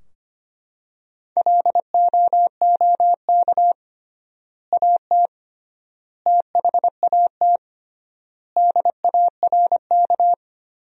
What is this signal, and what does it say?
Morse code, heard as a keyed tone, about 25 wpm, 700 Hz. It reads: LOOK AT THAT DARK
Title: 3x Word-Spacing - Sentences from Top 500 Words - 25wpm